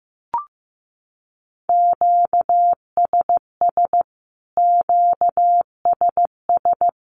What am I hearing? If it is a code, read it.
Morse code, heard as a keyed tone, about 15 wpm, 700 Hz. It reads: QSS QSS
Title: Q-Codes Repeated 3x 15wpm